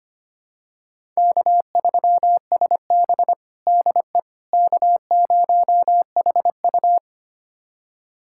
Morse code, heard as a keyed tone, about 25 wpm, 700 Hz. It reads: K3SB DE K0HU
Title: USA Callsign DE Callsign - 4 Characters - 25wpm